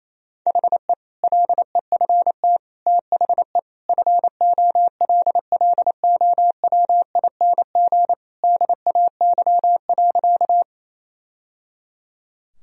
Morse code, heard as a keyed tone, about 28 wpm, 700 Hz. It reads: HE LEFT THE FOLLOWING DAY.
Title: Sentences from Top 1000 Words - Review Part 4 - 28wpm